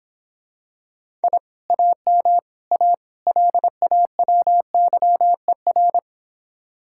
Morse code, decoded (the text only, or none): I AM A LAWYER